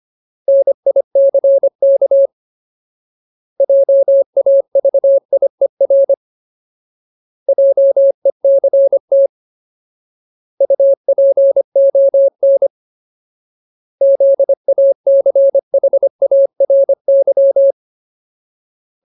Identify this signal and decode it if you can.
Morse code, heard as a keyed tone, about 25 wpm, 550 Hz. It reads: NICK JAVIER JECT UPON ZACHARY
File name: Ultimate Word List / 25WPM Ultimate